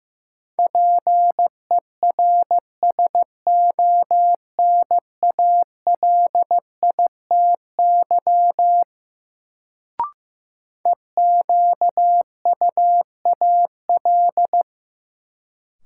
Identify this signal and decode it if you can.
Morse code, heard as a keyed tone, about 16 wpm, 700 Hz. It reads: PERSONALITY EQUAL
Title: Sentences from Top 1400 Words - New Word Drill Alt - 15wpm